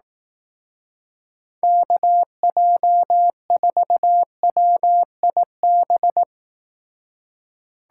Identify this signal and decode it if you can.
Morse code, heard as a keyed tone, about 18 wpm, 700 Hz. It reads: KJ4WIB
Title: Speed Racing - USA 2x3 Call Signs 15wpm